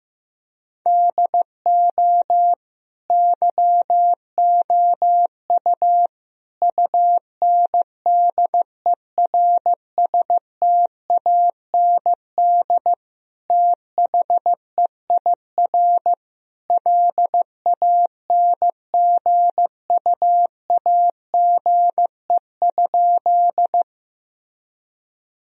Morse code, heard as a keyed tone, about 16 wpm, 700 Hz. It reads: DO YOU UNDERSTAND THEIR LANGUAGE?
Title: Sentences from Top 1000 Words - Review Part 3 - 15wpm